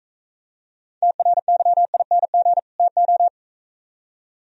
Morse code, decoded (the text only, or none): TRYING TO